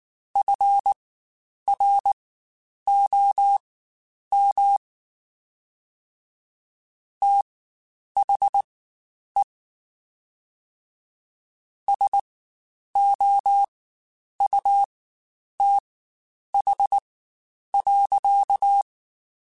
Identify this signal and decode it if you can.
Morse code, heard as a keyed tone, about 19 wpm, spaced as for 9 wpm, 800 Hz. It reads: FROM THE SOUTH.
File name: Spaced Characters / Spaced Characters-mp3s / Weather 4 10 sp @18